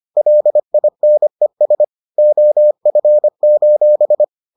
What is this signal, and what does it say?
Morse code, heard as a keyed tone, about 25 wpm, 600 Hz. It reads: LINES OF: